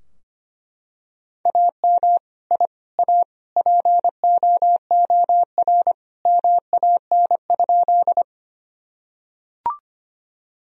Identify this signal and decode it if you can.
Morse code, heard as a keyed tone, about 25 wpm, 700 Hz. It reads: AM I A POOR MAN?
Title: Sentences from Top 1000 Words - Review Part 2 - 25wpm